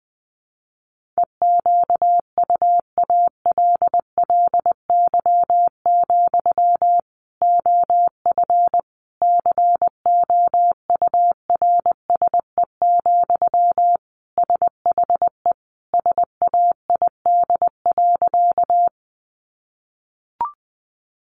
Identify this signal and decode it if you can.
Morse code, heard as a keyed tone, about 20 wpm, 700 Hz. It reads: EQUALLY, OF COURSE, SHE SAID.
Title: Sentences from Top 1400 Words 20wpm